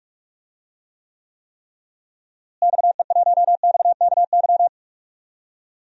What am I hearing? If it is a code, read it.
Morse code, heard as a keyed tone, about 46 wpm, 700 Hz. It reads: XE1XKY
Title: DXCC-100 Call Signs - 6 Characters - Repeated 3x - 45wpm